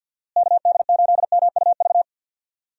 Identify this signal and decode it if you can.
Morse code, heard as a keyed tone, about 48 wpm, 700 Hz. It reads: KD8MUV
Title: USA Call Signs - 6 Characters - Repeated 3x - 50wpm